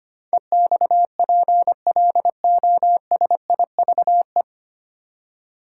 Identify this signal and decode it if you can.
Morse code, heard as a keyed tone, about 25 wpm, 700 Hz. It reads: EXPLOSIVE
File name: Nine Letter Words - 25wpm